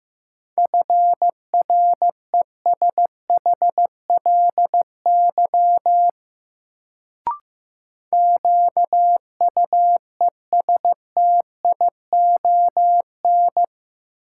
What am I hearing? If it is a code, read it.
Morse code, heard as a keyed tone, about 15 wpm, 700 Hz. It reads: FRESHLY QUESTION
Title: Sets of 5 Words Encore - New Word Drill - 15wpm